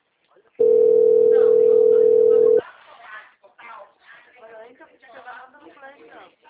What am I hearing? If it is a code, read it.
Ringback tone (North America), 440 + 480 Hz, with speech.
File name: out- -2007